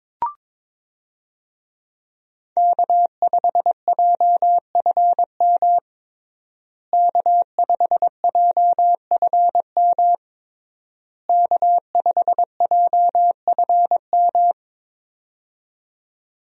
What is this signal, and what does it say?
Morse code, heard as a keyed tone, about 22 wpm, 700 Hz. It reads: K5JFM K5JFM K5JFM
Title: USA Call Signs - 5 Characters - Repeated 3x - 22wpm